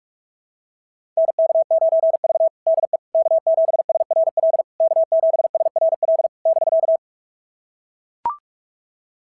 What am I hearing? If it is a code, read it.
Morse code, heard as a keyed tone, about 44 wpm, 690 Hz. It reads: NK9V DE K7SRL K7SRL <BK>
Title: SOTA QSO Practice 45wpm